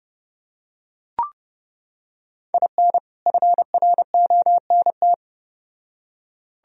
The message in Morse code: IN FRONT

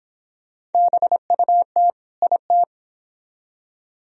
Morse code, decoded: BUT IT

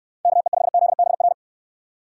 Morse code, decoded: KE5ZBR

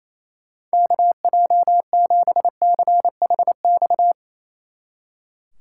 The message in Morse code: KJ7CHX

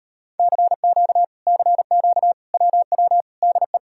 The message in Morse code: CQ CQ WW DE

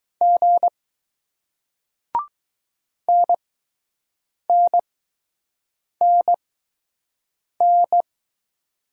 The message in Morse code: G NNNN

